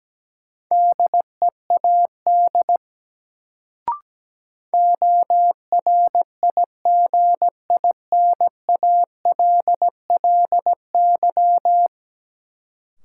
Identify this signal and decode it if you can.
Morse code, heard as a keyed tone, about 17 wpm, 700 Hz. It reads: DEAD ORIGINALLY
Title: Sentences from Top 1500 Words - New Word Drill Alt - 17wpm